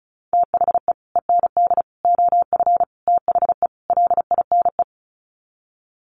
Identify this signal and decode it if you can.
Morse code, heard as a keyed tone, about 34 wpm, 700 Hz. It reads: THE END OF THE LINE
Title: Sets of 5 Words Encore2 35wpm